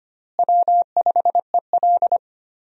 Morse code, heard as a keyed tone, about 25 wpm, 700 Hz. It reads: W5EL
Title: USA Call Signs - 4 Characters - Repeated 3x - 25wpm